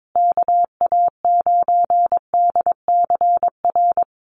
Morse code, heard as a keyed tone, about 22 wpm, 700 Hz. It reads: KA9DCR
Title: USA Call Signs Repeated 2x 22wpm